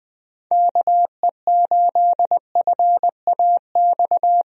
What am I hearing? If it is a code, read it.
Morse code, heard as a keyed tone, about 21 wpm, 700 Hz. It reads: KE8FAX
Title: USA Call Signs - 6 Characters - Repeated 3x - 20wpm